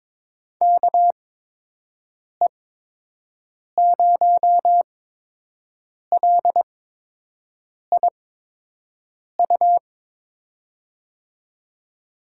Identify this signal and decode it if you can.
Morse code, heard as a keyed tone, about 22 wpm, spaced as for 6 wpm, 700 Hz. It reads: KE0LIU
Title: USA Call Signs - 6 Characters - 8x Character Spacing - 22wpm